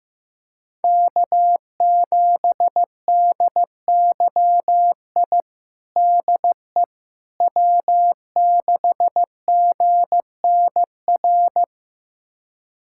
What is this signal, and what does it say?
Morse code, heard as a keyed tone, about 15 wpm, 700 Hz. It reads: K7DYI DE W6GNR
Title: USA Callsign DE Callsign - 5 Characters - 15wpm